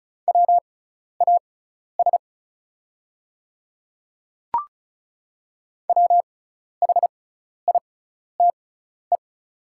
Morse code, heard as a keyed tone, about 35 wpm, spaced as for 12 wpm, 690 Hz. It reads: WAS WHITE
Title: 6x Character Spacing - Top 300 Words - 35wpm